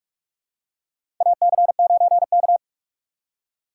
Morse code, heard as a keyed tone, about 44 wpm, 700 Hz. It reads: AC9K
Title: USA Call Signs - 4 Characters - 2x Character Spacing - 45wpm